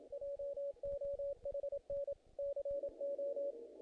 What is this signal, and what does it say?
Morse code, heard as a keyed tone, about 27 wpm, 560 Hz. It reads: JOHN CO
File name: contests / naqp-cw-k5tr-op-k5pi- -1900